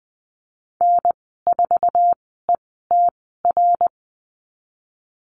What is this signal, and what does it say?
Morse code, heard as a keyed tone, about 20 wpm, spaced as for 14 wpm, 700 Hz. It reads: N4ETR
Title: 076 Lesson 13g - Letter L - Call Signs 20wpm